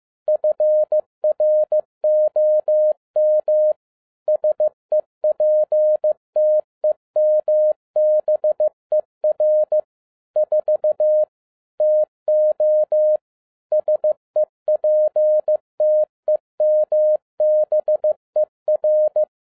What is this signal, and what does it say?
Morse code, heard as a keyed tone, about 15 wpm, 610 Hz. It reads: FROM SEPTEMBER 4 TO SEPTEMBER
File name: 2011-10-mp3 / n12-JL3YMV 15wpm